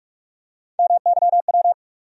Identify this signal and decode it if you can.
Morse code, heard as a keyed tone, about 46 wpm, 700 Hz. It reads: MYW
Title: USA Call Sign Postfixes - 3 Letters 45wpm